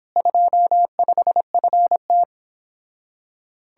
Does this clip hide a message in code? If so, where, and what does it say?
Yes: Morse code from 0.2 to 2.2 s, heard as a keyed tone, about 29 wpm, 700 Hz. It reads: 25FT